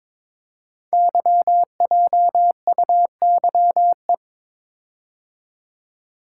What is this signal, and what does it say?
Morse code, heard as a keyed tone, about 22 wpm, 700 Hz. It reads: YJUYE
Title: Code Groups - 5 Letters - 22wpm